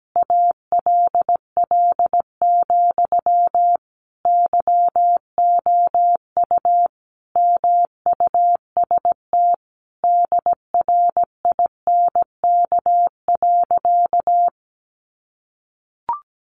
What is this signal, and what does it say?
Morse code, heard as a keyed tone, about 17 wpm, 700 Hz. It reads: ALL, YOU MUST DRINK.
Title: Sentences from Top 1500 Words 17wpm